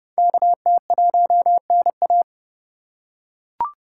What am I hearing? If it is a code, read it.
Morse code, heard as a keyed tone, about 29 wpm, 700 Hz. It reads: KT1NA